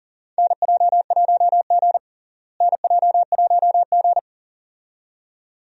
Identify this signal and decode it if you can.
Morse code, heard as a keyed tone, about 40 wpm, 700 Hz. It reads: NJ1G NJ1G
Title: USA Call Signs - 4 Characters - Repeated 2x - 40wpm